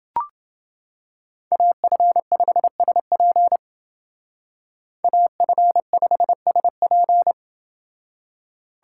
Morse code, heard as a keyed tone, about 28 wpm, 700 Hz. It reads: AF5SP AF5SP